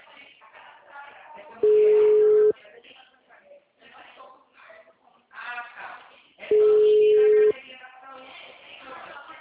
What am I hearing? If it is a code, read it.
Ringback tone (Europe, 425 Hz), with speech.
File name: out- -2004- - - .27442